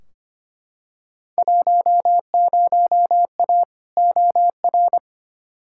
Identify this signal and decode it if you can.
Morse code, heard as a keyed tone, about 25 wpm, 700 Hz. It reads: 10A OR
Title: Field Day Exchanges - 25wpm